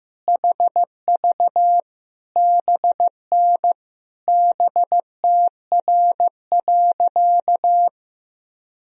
Morse code, heard as a keyed tone, about 15 wpm, 700 Hz. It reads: HV BN BTR.
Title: Sentences from Top 500 Words and Abbreviations 15wpm